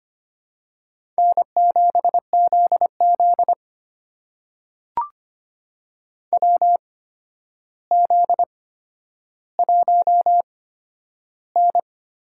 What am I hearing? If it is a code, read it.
Morse code, heard as a keyed tone, about 25 wpm, spaced as for 11 wpm, 700 Hz. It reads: N7ZZ WZ1N
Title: USA Call Signs - 4 Characters - 8x Character Spacing - 25wpm